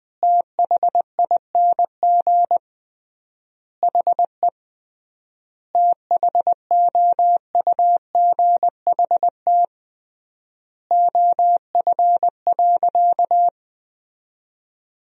Morse code, heard as a keyed tone, about 20 wpm, 700 Hz. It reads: THING HE THOUGHT OF.
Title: 3x Word-Spacing - Sentences from Top 300 Words - 20wpm